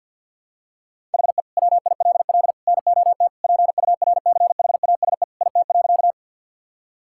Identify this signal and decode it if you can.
Morse code, heard as a keyed tone, about 52 wpm, 700 Hz. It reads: HE WILL NOT PURCHASE IT.
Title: Sentences from Top 1000 Words - Review Part 5 - 50wpm